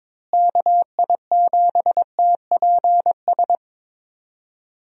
Morse code, heard as a keyed tone, about 22 wpm, 700 Hz. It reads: KI7TPS
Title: USA Call Signs - 6 Characters - Repeated 2x - 22wpm